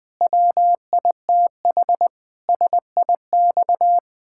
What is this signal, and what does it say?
Morse code, heard as a keyed tone, about 21 wpm, 700 Hz. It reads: WITH SIX
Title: Pangram Challenge v2 - 20wpm